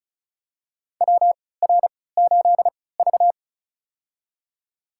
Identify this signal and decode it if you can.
Morse code, heard as a keyed tone, about 36 wpm, spaced as for 20 wpm, 700 Hz. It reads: WR8V
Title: USA Call Signs - 4 Characters - 3x Character Spacing - 35wpm